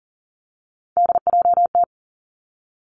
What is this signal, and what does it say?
Morse code, heard as a keyed tone, about 38 wpm, 690 Hz. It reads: DJT
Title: Three Challenging Letter-Number Combinations 40wpm